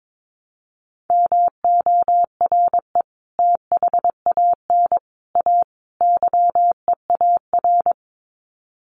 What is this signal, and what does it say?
Morse code, heard as a keyed tone, about 22 wpm, 700 Hz. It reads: MORE THAN A YEAR
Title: Sets of 4 Words 22wpm